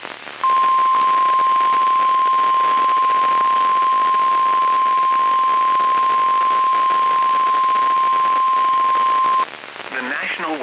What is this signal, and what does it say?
Emergency Alert System (SAME) tones (the 1050 Hz attention tone), with speech. It is heard from 0.4 to 9.5 s.